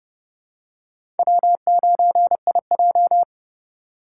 Morse code, heard as a keyed tone, about 30 wpm, 700 Hz. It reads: W9IJ
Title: USA Call Signs - 2x Character Spacing - 30wpm